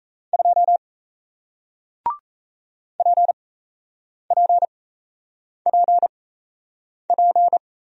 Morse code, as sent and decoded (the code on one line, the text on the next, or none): ..--- / .--. .--. .--. .--.
2 PPPP